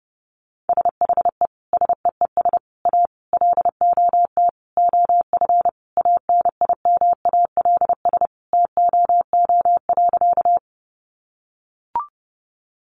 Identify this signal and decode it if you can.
Morse code, heard as a keyed tone, about 30 wpm, 700 Hz. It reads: SHE SEES A LOT OF ANIMALS TOO.
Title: Easy Phrases 30wpm